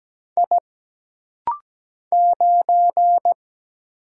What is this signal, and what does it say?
Morse code, heard as a keyed tone, about 17 wpm, 700 Hz. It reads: I 9
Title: Single Letter-Number - Rapid-Fire 17wpm